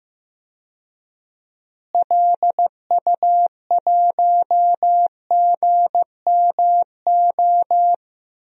Morse code, heard as a keyed tone, about 15 wpm, 700 Hz. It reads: LU1GMO
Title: DXCC-100 Call Signs - 6 Characters - Repeated 2x - 15wpm